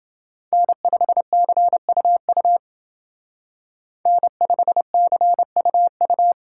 Morse code, heard as a keyed tone, about 29 wpm, 700 Hz. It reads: N5CUU N5CUU